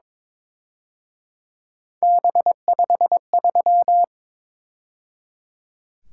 Morse code, heard as a keyed tone, about 26 wpm, 700 Hz. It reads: B53
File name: Speed-Racing - Three Letter-Number Combinations - 22wpm